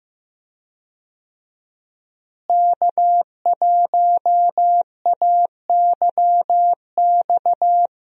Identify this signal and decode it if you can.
Morse code, heard as a keyed tone, about 15 wpm, 700 Hz. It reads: K1AYX
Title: USA Call Signs - 5 Characters - Repeated 3x - 15wpm